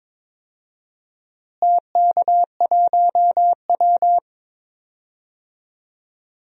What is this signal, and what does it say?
Morse code, heard as a keyed tone, about 22 wpm, 700 Hz. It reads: TK1W
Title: DXCC-100 Call Signs - 4 Characters - Repeated 2x - 22wpm